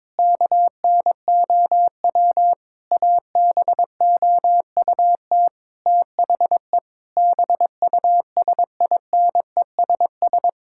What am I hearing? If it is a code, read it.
Morse code, heard as a keyed tone, about 22 wpm, 700 Hz. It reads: KNOW ABOUT THE BUSINESS
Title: Sentences from Top 1000 Words - Review Part 6 - 22wpm